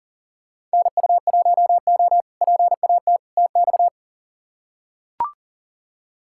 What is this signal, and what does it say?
Morse code, heard as a keyed tone, about 40 wpm, 700 Hz. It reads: NU1O PAT TX
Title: K1USN SST QSO Practice - 40wpm